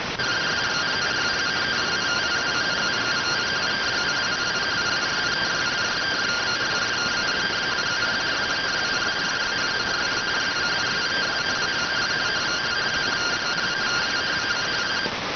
Two-tone teleprinter signal, about 170 Hz shift.